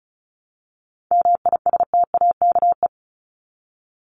Morse code, heard as a keyed tone, about 34 wpm, 700 Hz. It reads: MISTAKE
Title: Top 1501-1750 Words 35wpm